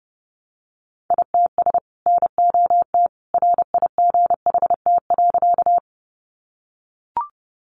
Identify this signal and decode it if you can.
Morse code, heard as a keyed tone, about 30 wpm, 700 Hz. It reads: ITS NOT RIGHT.
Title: Sentences from Top 600 Words 30wpm